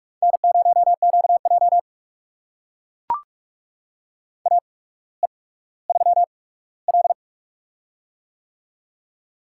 Morse code, heard as a keyed tone, about 44 wpm, spaced as for 19 wpm, 690 Hz. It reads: N0QJ AE3L
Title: USA Call Signs - 4 Characters - 8x Character Spacing - 45wpm